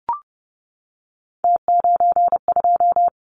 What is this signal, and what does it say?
Morse code, heard as a keyed tone, about 30 wpm, 700 Hz. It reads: T92
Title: Callsign Prefixes for Top 100 DXCC - 30wpm